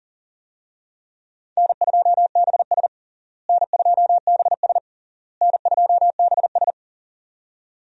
Morse code, heard as a keyed tone, about 40 wpm, 700 Hz. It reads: N2BS N2BS N2BS
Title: USA Call Signs - 4 Characters - Repeated 3x - 40wpm